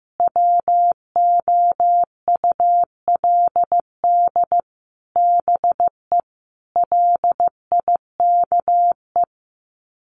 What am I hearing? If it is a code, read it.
Morse code, heard as a keyed tone, about 15 wpm, 700 Hz. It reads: WOULD BE LIKE
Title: Sets of 5 Words Encore 15wpm